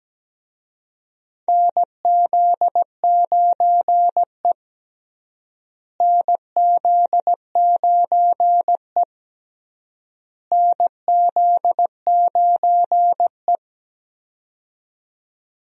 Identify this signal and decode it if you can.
Morse code, heard as a keyed tone, about 17 wpm, 700 Hz. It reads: NZ9E NZ9E NZ9E
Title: USA Call Signs - 4 Characters - Repeated 3x - 17wpm